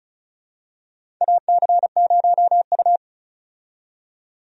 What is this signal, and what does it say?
Morse code, heard as a keyed tone, about 34 wpm, 700 Hz. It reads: AC0U
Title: USA Call Signs - 4 Characters - 35wpm